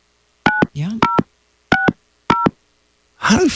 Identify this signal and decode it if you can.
DTMF, heard as telephone touch tones, tones of about 150 ms, with speech. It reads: 9060